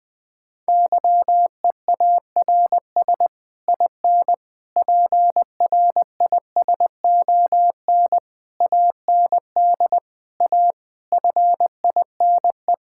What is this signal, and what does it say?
Morse code, heard as a keyed tone, about 20 wpm, 700 Hz. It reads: YEARS IN PRISON AND A FINE
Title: Sets of 9 Words - 20wpm